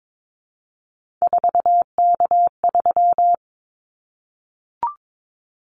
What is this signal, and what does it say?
Morse code, heard as a keyed tone, about 22 wpm, 700 Hz. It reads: 4K3
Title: Callsign Prefixes for Top 100 DXCC - 22wpm